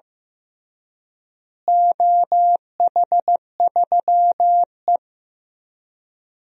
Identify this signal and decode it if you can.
Morse code, heard as a keyed tone, about 15 wpm, 700 Hz. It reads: OH3E
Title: DXCC-100 Call Signs - 4 Characters - Repeated 3x - 15wpm